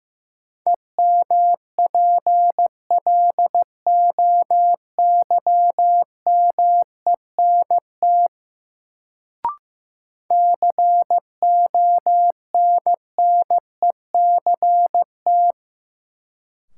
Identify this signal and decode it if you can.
Morse code, heard as a keyed tone, about 16 wpm, 700 Hz. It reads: EMPLOYMENT CONNECT
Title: Sentences from Top 1500 Words - New Word Drill - 15wpm